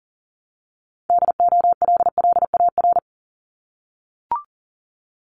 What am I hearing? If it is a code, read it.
Morse code, heard as a keyed tone, about 40 wpm, 700 Hz. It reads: DOLLAR
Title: Multi-letter Phonemes 40wpm